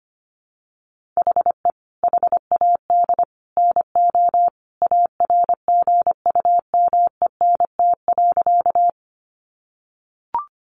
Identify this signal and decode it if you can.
Morse code, heard as a keyed tone, about 25 wpm, 700 Hz. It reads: HE HAD NO ARGUMENT.
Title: Sentences from Top 1400 Words 25wpm